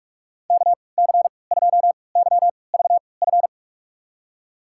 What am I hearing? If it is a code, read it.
Morse code, heard as a keyed tone, about 46 wpm, spaced as for 26 wpm, 700 Hz. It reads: KC2YVF
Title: USA Call Signs - 3x Character Spacing - 45wpm